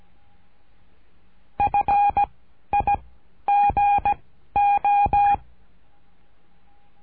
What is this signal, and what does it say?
Morse code, heard as a keyed tone, about 17 wpm, 810 Hz. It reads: F I G O
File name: cursos / CURSO DE TELEGRAFIA / Telegrafia - Trilha 13